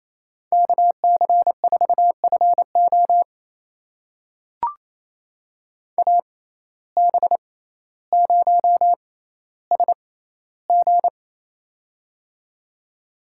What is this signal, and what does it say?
Morse code, heard as a keyed tone, about 28 wpm, spaced as for 14 wpm, 700 Hz. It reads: KC4FO AB0SG
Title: USA Call Signs - 5 Characters - 6x Character Spacing - 28wpm